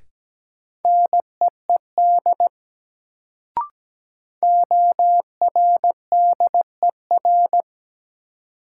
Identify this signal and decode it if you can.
Morse code, heard as a keyed tone, about 17 wpm, 700 Hz. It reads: NEED ORDER
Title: Sets of 2 Words - New Word Drill - 17wpm